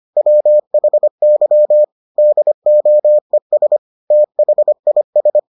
Morse code, heard as a keyed tone, about 25 wpm, 600 Hz. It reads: WHY DOES THIS